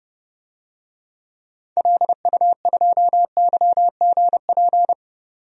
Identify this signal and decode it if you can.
Morse code, heard as a keyed tone, about 32 wpm, 700 Hz. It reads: LU2YGP
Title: DXCC-100 Call Signs - 6 Characters - 30wpm